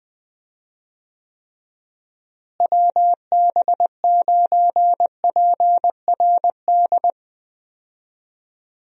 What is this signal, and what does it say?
Morse code, heard as a keyed tone, about 20 wpm, 700 Hz. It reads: WB9PRD
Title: USA Call Signs - 6 Characters - Repeated 2x - 20wpm